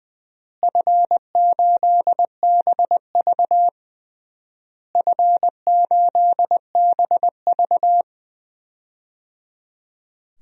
Morse code, heard as a keyed tone, about 20 wpm, 700 Hz. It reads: F8BV F8BV
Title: DXCC-100 Call Signs - 4 Characters - Repeated 3x - 20wpm